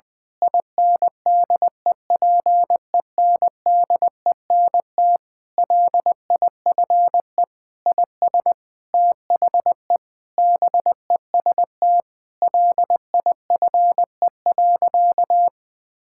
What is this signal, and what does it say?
Morse code, heard as a keyed tone, about 21 wpm, 700 Hz. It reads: INDEPENDENT LIFE IS THE BEST LIFE.
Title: Sentences from Top 1000 Words - Review Part 6 - 20wpm